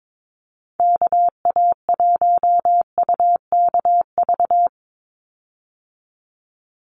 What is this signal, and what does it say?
Morse code, heard as a keyed tone, about 22 wpm, 700 Hz. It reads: KA1UKV
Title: USA Call Signs 22wpm